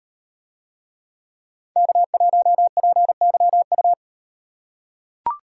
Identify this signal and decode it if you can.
Morse code, heard as a keyed tone, about 38 wpm, 700 Hz. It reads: K1PYU